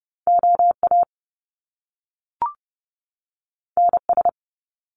Morse code, heard as a keyed tone, about 30 wpm, 700 Hz. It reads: OA NS
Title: Top Two Letter Combinations in Words 30wpm